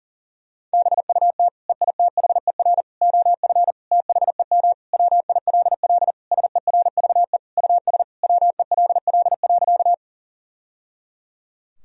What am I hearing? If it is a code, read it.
Morse code, heard as a keyed tone, about 37 wpm, 700 Hz. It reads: BUT EITHER OF THEM WILL SERVE US WELL.
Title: Sentences from Top 1000 Words - Review Part 4 - 40wpm